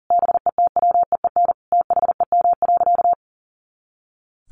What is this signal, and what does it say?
Morse code, heard as a keyed tone, about 37 wpm, 700 Hz. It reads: BETWEEN THEM.
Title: Sentences from Top 400 Words 40wpm